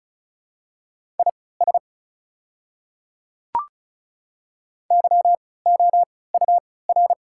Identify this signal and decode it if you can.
Morse code, heard as a keyed tone, about 36 wpm, spaced as for 20 wpm, 690 Hz. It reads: IS YOUR
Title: 3x Character Spacing - Top 200 Words - 35wpm